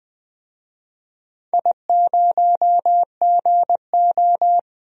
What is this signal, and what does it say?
Morse code, heard as a keyed tone, about 20 wpm, 700 Hz. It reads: I0GO
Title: DXCC-100 Call Signs - 4 Characters - 20wpm